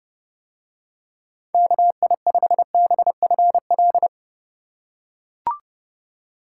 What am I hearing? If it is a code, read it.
Morse code, heard as a keyed tone, about 27 wpm, 700 Hz. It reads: KI5BFL